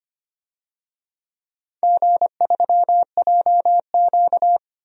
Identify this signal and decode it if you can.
Morse code, heard as a keyed tone, about 25 wpm, 700 Hz. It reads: G3JQ
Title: DXCC-100 Call Signs - 4 Characters - Repeated 3x - 25wpm